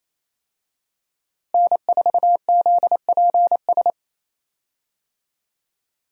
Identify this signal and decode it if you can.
Morse code, heard as a keyed tone, about 27 wpm, 700 Hz. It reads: N4ZPS
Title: USA Call Signs - 5 Characters - Repeated 3x - 28wpm